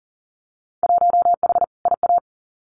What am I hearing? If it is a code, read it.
Morse code, heard as a keyed tone, about 40 wpm, 700 Hz. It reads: 1H IA
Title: Winter Field Day Exchanges 40wpm